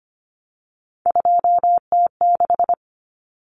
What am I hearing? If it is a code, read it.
Morse code, heard as a keyed tone, about 25 wpm, 700 Hz. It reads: 2T6
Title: Common Cut Numbers - 3-Digit 25wpm